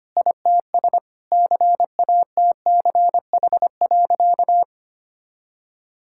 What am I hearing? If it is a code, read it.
Morse code, heard as a keyed tone, about 25 wpm, 700 Hz. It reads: ITS CATCH.
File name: practice / 25wpm / Sentences with 2-Word Phrases - 25wpm